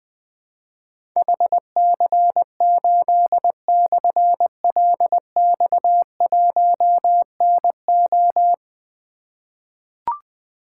Morse code, heard as a keyed tone, about 20 wpm, 700 Hz. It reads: HC8/LX1NO
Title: Difficult Call Signs 20wpm